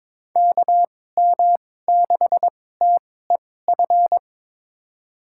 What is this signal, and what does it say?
Morse code, heard as a keyed tone, about 22 wpm, spaced as for 16 wpm, 700 Hz. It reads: KM6TEF
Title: USA Call Signs - 6 Characters - 2x Character Spacing - 22wpm